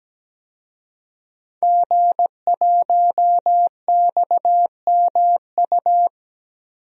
Morse code, heard as a keyed tone, about 17 wpm, 700 Hz. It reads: G1XMU
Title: DXCC-100 Call Signs - 5 Characters - 17wpm